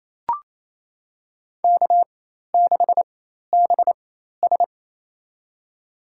Morse code, heard as a keyed tone, about 28 wpm, spaced as for 13 wpm, 700 Hz. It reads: K6BS
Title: USA Call Signs - 4 Characters - 4x Character Spacing - 28wpm